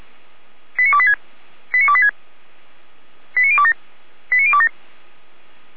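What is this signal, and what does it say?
Five-tone selective call (ZVEI), heard as quick runs of five tones, tones of about 70 ms. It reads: 89286 89286 89027 89027
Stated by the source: ZVEI five-tone selective call 89220 received by SDR, from the Feuerwehr Orlamünde audio archive.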